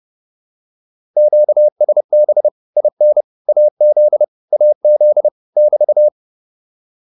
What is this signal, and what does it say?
Morse code, heard as a keyed tone, about 30 wpm, 600 Hz. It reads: QSB IN AZ AZ <BT>